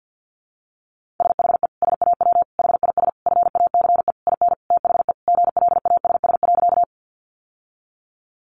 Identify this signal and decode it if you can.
Morse code, heard as a keyed tone, about 50 wpm, 700 Hz. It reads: SHE SAW HIS FACE IN THE GLASS.
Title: Sentences from Top 1300 Words 50wpm